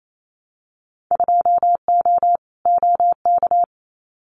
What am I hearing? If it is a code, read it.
Morse code, heard as a keyed tone, about 28 wpm, 700 Hz. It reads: 2O OK
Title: Winter Field Day Exchanges 28wpm